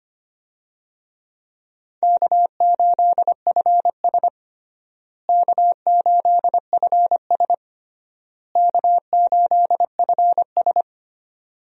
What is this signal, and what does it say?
Morse code, heard as a keyed tone, about 25 wpm, 700 Hz. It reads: K8FS K8FS K8FS
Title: USA Call Signs - 4 Characters - Repeated 3x - 25wpm